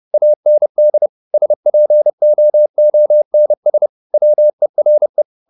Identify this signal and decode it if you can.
Morse code, heard as a keyed tone, about 30 wpm, 600 Hz. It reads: AND SPOONS WERE
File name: the old man and the sea / the old man and the sea 1 30wpm